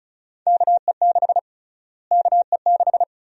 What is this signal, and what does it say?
Morse code, heard as a keyed tone, about 36 wpm, 700 Hz. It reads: KE6 KE6
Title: USA Call Sign Prefixes - Repeated 2x - 35wpm